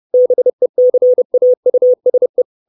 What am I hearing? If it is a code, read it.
Morse code, heard as a keyed tone, about 30 wpm, 500 Hz. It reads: BECAUSE
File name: 632 Words 30WPM